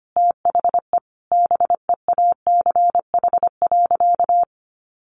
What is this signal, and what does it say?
Morse code, heard as a keyed tone, about 25 wpm, 700 Hz. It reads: THE BEACH.
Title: Sentences from Top 1500 Words 25wpm